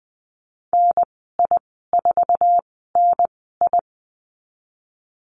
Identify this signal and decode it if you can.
Morse code, heard as a keyed tone, about 20 wpm, spaced as for 14 wpm, 700 Hz. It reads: NI4NI
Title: 076 Lesson 13g - Letter L - Call Signs 20wpm